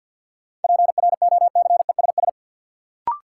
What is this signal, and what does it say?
Morse code, heard as a keyed tone, about 53 wpm, 700 Hz. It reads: PROCESS
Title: Sentences from Top 1000 Words - Review Part 3 - New Word Drill - 50wpm